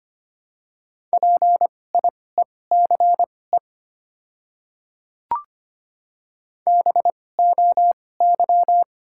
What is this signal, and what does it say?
Morse code, heard as a keyed tone, about 25 wpm, spaced as for 18 wpm, 690 Hz. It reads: PIECE BOY
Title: 2x Character Spacing - Top 500 Words - 25wpm